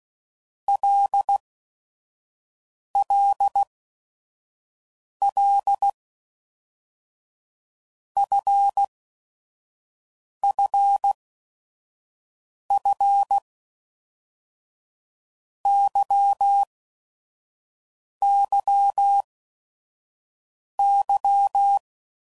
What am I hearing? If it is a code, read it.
Morse code, heard as a keyed tone, about 16 wpm, 800 Hz. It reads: L L L F F F Y Y Y
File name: Basic Letter Groups By7Groups / LFCYQ / LFCYQ@5 f15